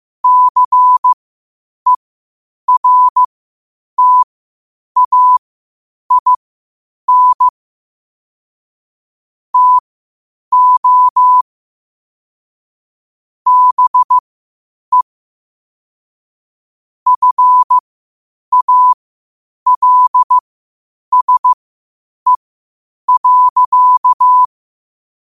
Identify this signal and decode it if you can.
Morse code, heard as a keyed tone, about 15 wpm, spaced as for 8 wpm, 1000 Hz. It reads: CERTAIN TO BE FALSE.
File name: qotdfiles / 8wpm